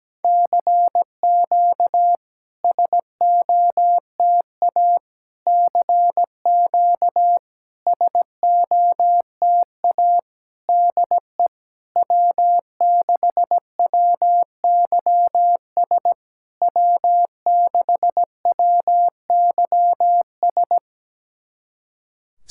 Morse code, heard as a keyed tone, about 17 wpm, 700 Hz. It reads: CQ SOTA CQ SOTA DE W6WYS W6WYS
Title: SOTA QSO Practice 17wpm